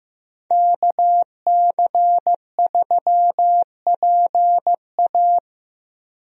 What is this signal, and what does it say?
Morse code, heard as a keyed tone, about 15 wpm, 700 Hz. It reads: KC3PA
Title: USA Call Signs - 5 Characters - Repeated 3x - 15wpm